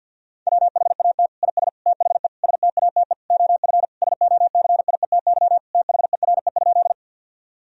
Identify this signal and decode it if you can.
Morse code, heard as a keyed tone, about 50 wpm, 700 Hz. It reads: WHAT IS THE STATE OF SOCIETY THERE?
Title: Sentences from Top 1000 Words - Review Part 4 - 50wpm